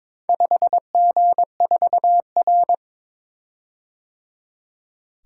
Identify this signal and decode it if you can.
Morse code, heard as a keyed tone, about 22 wpm, 690 Hz. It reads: 5G4R
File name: practice / 22wpm / Four Letter-Number Combinations - 22wpm